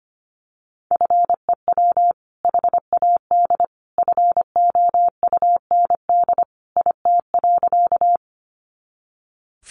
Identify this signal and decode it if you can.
Morse code, heard as a keyed tone, about 25 wpm, 700 Hz. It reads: FEW HAD FOUND IT.
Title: Sentences from Top 300 Words 25wpm